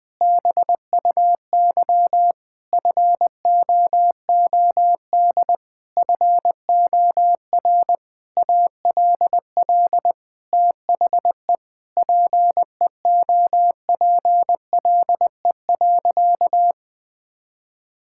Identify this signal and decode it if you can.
Morse code, heard as a keyed tone, about 20 wpm, 700 Hz. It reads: BUY FOOD FOR ALL THE PEOPLE.
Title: Sentences from Top 1600 Words 20wpm